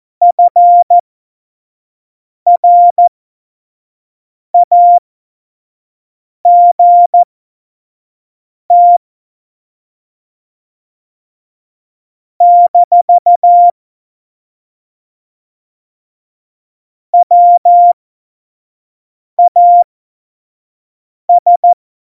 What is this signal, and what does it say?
Morse code, heard as a keyed tone, about 14 wpm, spaced as for 5 wpm, 700 Hz. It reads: FRAGT - WAS
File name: Morsekurs / Witze / Witz 09. April 2019, 25 F70